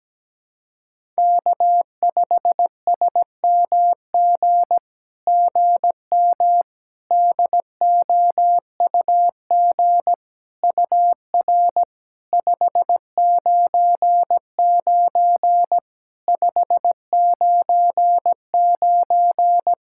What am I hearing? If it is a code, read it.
Morse code, heard as a keyed tone, about 17 wpm, 700 Hz. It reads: K5SMG GM DOUG UR 599 599
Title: SOTA QSO Practice 17wpm